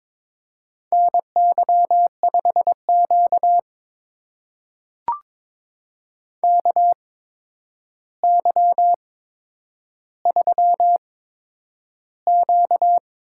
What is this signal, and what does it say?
Morse code, heard as a keyed tone, about 22 wpm, spaced as for 9 wpm, 700 Hz. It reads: NY5Q KY3Q
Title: USA Call Signs - 4 Characters - 8x Character Spacing - 22wpm